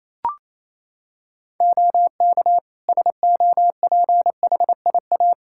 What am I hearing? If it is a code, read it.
Morse code, heard as a keyed tone, about 28 wpm, 700 Hz. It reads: OK SOPHIA
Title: Practice QSO - Exchange 3 - More Info - 28wpm